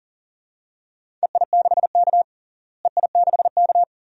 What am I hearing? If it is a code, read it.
Morse code, heard as a keyed tone, about 40 wpm, 700 Hz. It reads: EI6K EI6K
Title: DXCC-100 Call Signs - 4 Characters - Repeated 2x - 40wpm